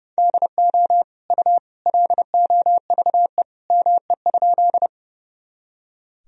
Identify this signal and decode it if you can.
Morse code, heard as a keyed tone, about 30 wpm, 700 Hz. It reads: DO U LOVE ME?
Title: Sentences from Top 400 Words and Abbreviations 30wpm